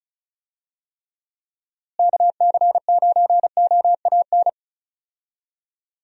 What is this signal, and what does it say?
Morse code, heard as a keyed tone, about 36 wpm, 700 Hz. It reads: KC9OAN
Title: USA Call Signs - 6 Characters - Repeated 3x - 35wpm